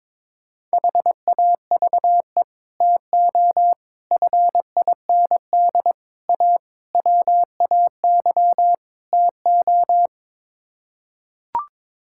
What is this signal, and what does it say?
Morse code, heard as a keyed tone, about 22 wpm, 700 Hz. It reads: HAVE TO FIND A WAY TO
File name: practice / 22wpm / Sets of 6 Words - 22wpm